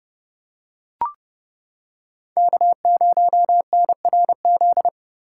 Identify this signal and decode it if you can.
Morse code, heard as a keyed tone, about 29 wpm, 700 Hz. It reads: K0NRZ